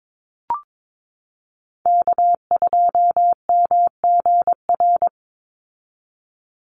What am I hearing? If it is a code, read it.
Morse code, heard as a keyed tone, about 22 wpm, 700 Hz. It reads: K2MGR
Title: Call Signs - 22wpm